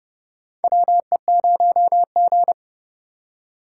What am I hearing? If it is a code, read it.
Morse code, heard as a keyed tone, about 30 wpm, 700 Hz. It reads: WE0G
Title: USA Call Signs - 4 Characters - 4x Character Spacing - 30wpm